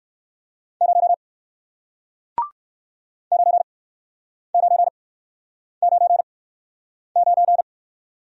Morse code, heard as a keyed tone, about 59 wpm, spaced as for 9 wpm, 700 Hz. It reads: 0 9999